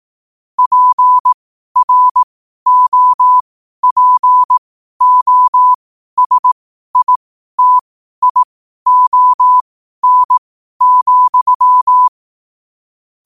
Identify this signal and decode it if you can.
Morse code, heard as a keyed tone, about 18 wpm, spaced as for 12 wpm, 1000 Hz. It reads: PROPOSITION,
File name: qotdfiles / 12wpm